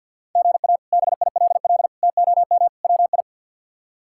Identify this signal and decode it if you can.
Morse code, heard as a keyed tone, about 53 wpm, 700 Hz. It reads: GA BILL TOM WI
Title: K1USN SST QSO Practice - 50wpm